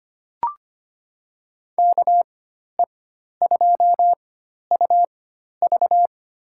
Morse code, heard as a keyed tone, about 25 wpm, spaced as for 12 wpm, 700 Hz. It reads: KE2UV
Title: USA Call Signs - 5 Characters - 4x Character Spacing - 25wpm